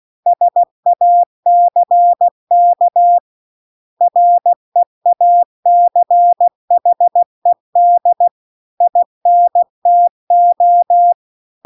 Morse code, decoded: SACK REACHED INTO

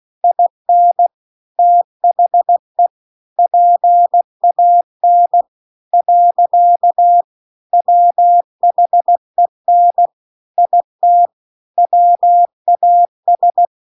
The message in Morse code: IN THE PAN . WHEN IT WAS

